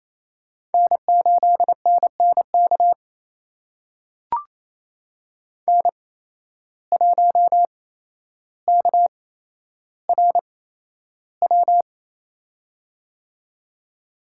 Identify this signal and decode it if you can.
Morse code, heard as a keyed tone, about 28 wpm, spaced as for 12 wpm, 700 Hz. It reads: N8NNK N1KRW
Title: USA Call Signs - 5 Characters - 8x Character Spacing - 28wpm